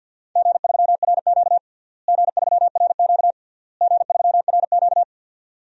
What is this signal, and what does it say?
Morse code, heard as a keyed tone, about 46 wpm, 700 Hz. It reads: G3RQ G3RQ G3RQ
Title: DXCC-100 Call Signs - 4 Characters - Repeated 3x - 50wpm